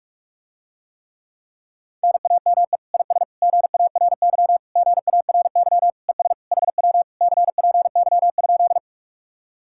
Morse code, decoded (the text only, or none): NAME IS GARY GARY ES HW CPY?